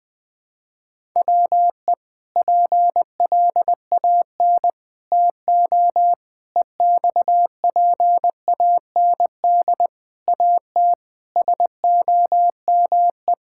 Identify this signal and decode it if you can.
Morse code, heard as a keyed tone, about 20 wpm, 700 Hz. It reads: WE PLAN TO EXPAND AT SOME
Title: Sentences with 6-Word Phrases - 20wpm